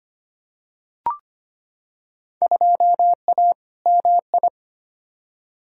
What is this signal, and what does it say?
Morse code, heard as a keyed tone, about 25 wpm, 700 Hz. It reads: 2A MI
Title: Field Day Exchanges Encore - 25wpm